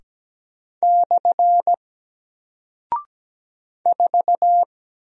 Morse code, heard as a keyed tone, about 17 wpm, 700 Hz. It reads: / 4
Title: Single Character (Letter:Number:Punctuation:Prosign) Alt 17wpm